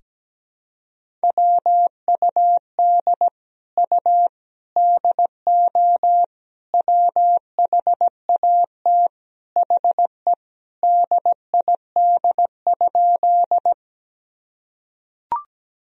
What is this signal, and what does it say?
Morse code, heard as a keyed tone, about 17 wpm, 700 Hz. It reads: WUD U DO WHAT HE DID?
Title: Sentences from Top 100 Words and Abbreviations 17wpm